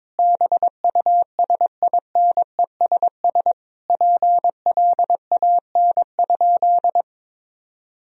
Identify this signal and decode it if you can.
Morse code, heard as a keyed tone, about 22 wpm, 700 Hz. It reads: BUSINESS PLAN?
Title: Sentences from Top 1000 Words - Review Part 3 - 22wpm